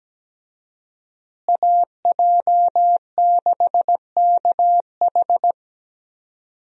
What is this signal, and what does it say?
Morse code, heard as a keyed tone, about 17 wpm, 700 Hz. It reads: AJ6KH
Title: USA Call Signs - 5 Characters - Repeated 3x - 17wpm